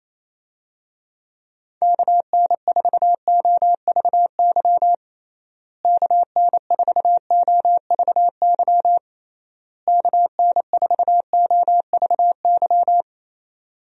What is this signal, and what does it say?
Morse code, heard as a keyed tone, about 28 wpm, 700 Hz. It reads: KN4OVY KN4OVY KN4OVY
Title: USA Call Signs - 6 Characters - Repeated 3x - 28wpm